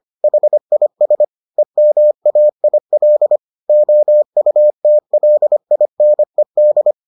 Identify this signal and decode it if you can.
Morse code, heard as a keyed tone, about 25 wpm, 600 Hz. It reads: HIS EMAIL OUTLINED